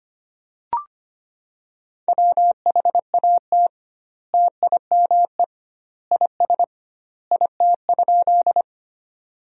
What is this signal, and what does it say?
Morse code, heard as a keyed tone, about 25 wpm, 700 Hz. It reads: WHAT TIME IS IT?
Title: 2x Word-Spacing - Sentences from Top 100 Words - 25wpm